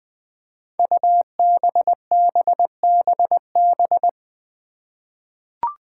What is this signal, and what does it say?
Morse code, heard as a keyed tone, about 21 wpm, 700 Hz. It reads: UBBBB
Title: Five Challenging Letter-Number Combinations 20wpm